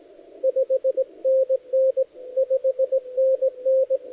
Morse code, heard as a keyed tone, about 18 wpm, 500 Hz, with radio noise. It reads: 5NN 5NN
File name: dlazaawansowanych / nagraniazpasma / titawaNETcontrol